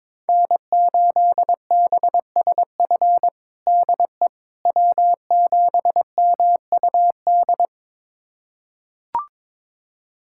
Morse code, heard as a keyed tone, about 22 wpm, 700 Hz. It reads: N8BSF DE W7MUD
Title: USA Callsign DE Callsign - 5 Characters - 22wpm